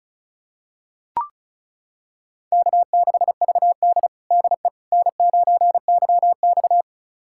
Morse code, heard as a keyed tone, about 34 wpm, 700 Hz. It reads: K6VD DE N9YX
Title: USA Callsign DE Callsign - 4 Characters - 35wpm